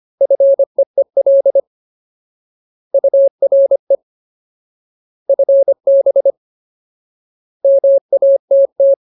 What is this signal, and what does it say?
Morse code, heard as a keyed tone, about 25 wpm, 550 Hz. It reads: FEEL URE FB MATT
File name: Ultimate Word List / 25WPM Ultimate